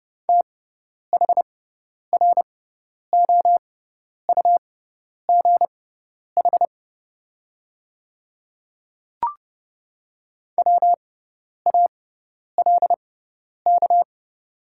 Morse code, heard as a keyed tone, about 31 wpm, spaced as for 10 wpm, 690 Hz. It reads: THROUGH WALK
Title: 6x Character Spacing - Top 300 Words - 30wpm